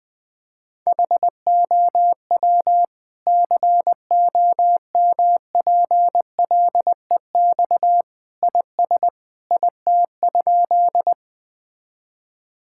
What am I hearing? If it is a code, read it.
Morse code, heard as a keyed tone, about 21 wpm, 700 Hz. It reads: HOW COMPLEX IS IT?
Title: Sentences from Top 1000 Words - Review Part 6 - 20wpm